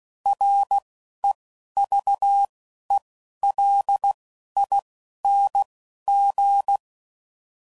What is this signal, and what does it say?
Morse code, heard as a keyed tone, about 16 wpm, 800 Hz. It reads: R E V E L I N G
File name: Prefixes / By First Letter / Rs